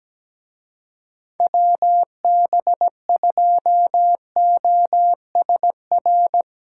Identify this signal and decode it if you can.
Morse code, heard as a keyed tone, about 19 wpm, 700 Hz. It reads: WB2OSR
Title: Speed Racing - USA 2x3 Call Signs 17wpm